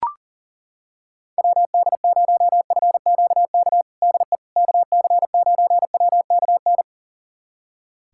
Morse code, heard as a keyed tone, about 40 wpm, 700 Hz. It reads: WD0FQK DE KC9WKN
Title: USA Callsign DE Callsign 40wpm